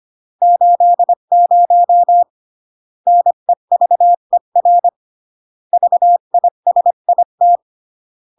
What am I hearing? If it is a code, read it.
Morse code, heard as a keyed tone, about 25 wpm, 700 Hz. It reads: 80 NEVER VISIT